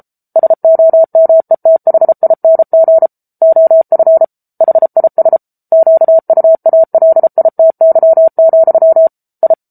Morse code, heard as a keyed tone, about 34 wpm, 650 Hz. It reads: SOMETHING OF HIS QUALITY, I
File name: Book Two / 07-Chapter 7 - 33 WPM